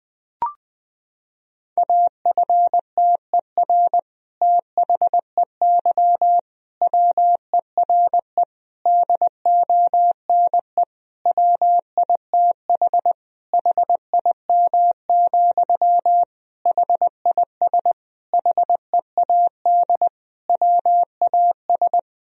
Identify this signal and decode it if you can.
Morse code, heard as a keyed tone, about 20 wpm, 700 Hz. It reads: AFTER THEY WERE DONE WITH HIM, HIS HEAD WAS
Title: Sentences from Top 1000 Words - Review Part 2 - 20wpm